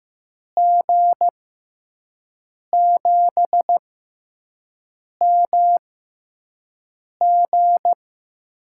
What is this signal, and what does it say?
Morse code, heard as a keyed tone, about 15 wpm, spaced as for 5 wpm, 700 Hz. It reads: G7MG
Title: USA Call Signs - 6x Character Spacing - 15wpm